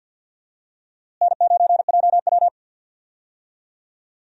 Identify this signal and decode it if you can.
Morse code, heard as a keyed tone, about 49 wpm, 700 Hz. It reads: N9JW
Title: USA Call Signs - 4 Characters - Repeated 3x - 50wpm